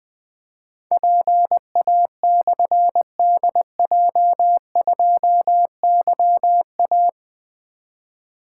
Morse code, heard as a keyed tone, about 20 wpm, 700 Hz. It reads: PA/DJ2YA
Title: Difficult Call Signs 20wpm